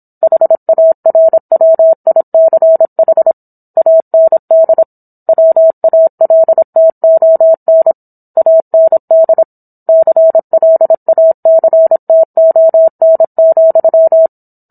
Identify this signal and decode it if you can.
Morse code, heard as a keyed tone, about 27 wpm, 650 Hz. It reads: HARWICH AND WALTON AND CLACTON,